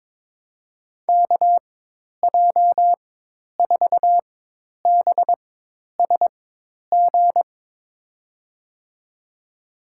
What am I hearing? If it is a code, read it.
Morse code, heard as a keyed tone, about 23 wpm, spaced as for 10 wpm, 700 Hz. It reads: KJ4BSG
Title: USA Call Signs - 4x Character Spacing - 22wpm